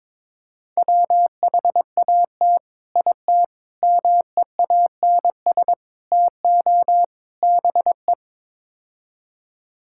Morse code, heard as a keyed tone, about 22 wpm, 700 Hz. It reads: WHAT IT MEANS TO BE
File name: practice / 22wpm / Build A Phrase - 5-Word Phrases - 22wpm